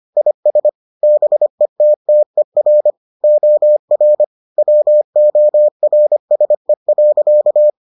Morse code, heard as a keyed tone, about 25 wpm, 600 Hz. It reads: IS BETTER OR WORSE.